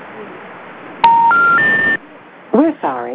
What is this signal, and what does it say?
Special information tone, with speech.